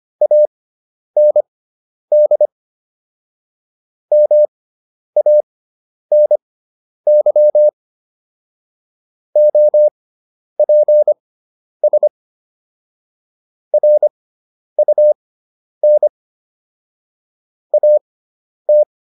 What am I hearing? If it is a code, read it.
Morse code, heard as a keyed tone, about 25 wpm, spaced as for 10 wpm, 600 Hz. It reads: AND MANY OPS RUN AT